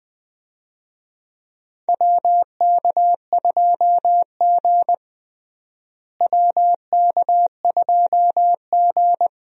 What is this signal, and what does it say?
Morse code, heard as a keyed tone, about 20 wpm, 700 Hz. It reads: WK2G WK2G
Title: USA Call Signs - 4 Characters - Repeated 3x - 20wpm